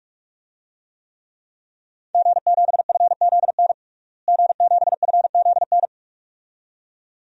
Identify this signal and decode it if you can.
Morse code, heard as a keyed tone, about 47 wpm, 700 Hz. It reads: G7FZN G7FZN
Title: DXCC-100 Call Signs - 5 Characters - Repeated 2x - 45wpm